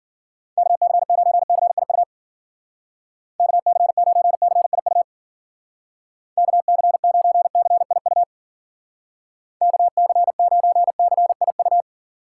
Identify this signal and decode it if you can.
Morse code, heard as a keyed tone, about 50 wpm, 700 Hz. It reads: KC9CIU KC9CIU KC9CIU KC9CIU
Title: Speed Racing - USA 2x3 Call Signs 40wpm